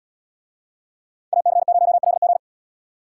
Morse code, heard as a keyed tone, about 68 wpm, 700 Hz. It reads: AC0LG